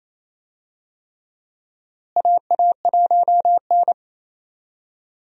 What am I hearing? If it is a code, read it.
Morse code, heard as a keyed tone, about 27 wpm, 700 Hz. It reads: AA1N
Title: USA Call Signs - 4 Characters - Repeated 3x - 28wpm